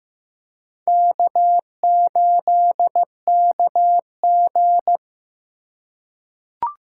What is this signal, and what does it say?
Morse code, heard as a keyed tone, about 15 wpm, 700 Hz. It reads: K8KG
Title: USA Call Signs - 4 Characters - 4x Character Spacing - 15wpm